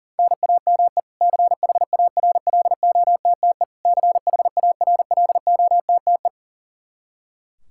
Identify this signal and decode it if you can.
Morse code, heard as a keyed tone, about 37 wpm, 700 Hz. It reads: NAME CHARLOTTE CHARLOTTE
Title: Practice QSO - Exchange 1 - Beginning - 40wpm